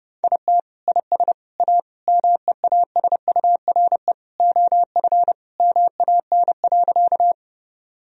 Morse code, heard as a keyed tone, about 30 wpm, 700 Hz. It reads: IT IS A MEASURE OF MAN.
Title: Sentences from Top 1000 Words - Review Part 2 - 30wpm